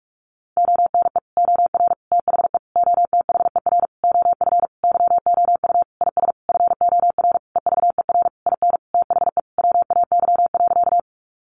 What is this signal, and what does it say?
Morse code, heard as a keyed tone, about 46 wpm, 700 Hz. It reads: ONE OR THE OTHER OF YOU IS FOR EVER IN THE WAY.
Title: Sentences from Top 400 Words 45wpm